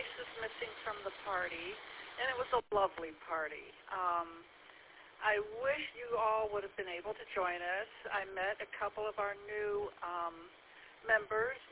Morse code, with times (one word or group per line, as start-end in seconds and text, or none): none